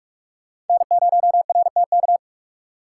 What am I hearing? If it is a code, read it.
Morse code, heard as a keyed tone, about 42 wpm, 700 Hz. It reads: N0RTK